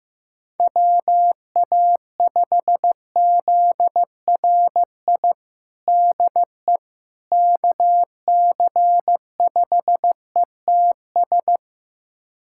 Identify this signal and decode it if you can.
Morse code, heard as a keyed tone, about 15 wpm, 700 Hz. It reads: WA5ZRI DE KC5ETS
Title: USA Callsign DE Callsign 15wpm